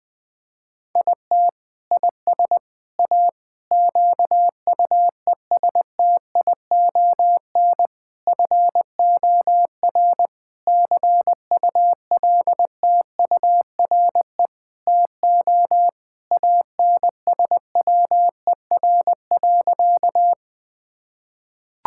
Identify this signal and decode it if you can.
Morse code, heard as a keyed tone, about 20 wpm, 700 Hz. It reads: IT IS A QUESTION FOR CULTURE TO ANSWER.
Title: Sentences from Top 1000 Words - Review Part 6 - 20wpm